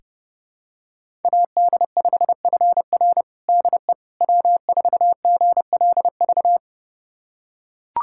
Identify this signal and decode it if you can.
Morse code, heard as a keyed tone, about 32 wpm, 700 Hz. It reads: AD5FR DE W4GLV
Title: USA Callsign DE Callsign - 5 Characters - 30wpm